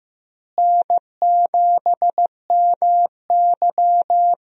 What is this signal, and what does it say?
Morse code, heard as a keyed tone, about 15 wpm, 700 Hz. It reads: N7MY
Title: USA Call Signs - 4 Characters - Repeated 2x - 15wpm